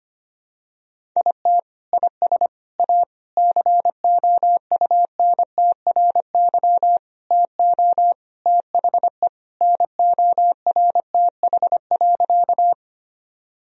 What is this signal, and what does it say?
Morse code, heard as a keyed tone, about 25 wpm, 700 Hz. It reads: IT IS A COUNTRY TO THE NORTH.
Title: Sentences from Top 1000 Words - Review Part 1 - 25wpm